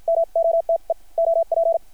Morse code, heard as a keyed tone, about 50 wpm, spaced as for 42 wpm, 660 Hz. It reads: MOTE OP
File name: archived / CRSnet